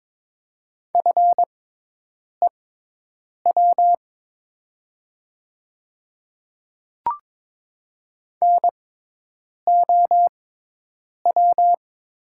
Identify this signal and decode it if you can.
Morse code, heard as a keyed tone, about 22 wpm, spaced as for 8 wpm, 700 Hz. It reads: FEW NOW
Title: 6x Character Spacing - Top 300 Words - 22wpm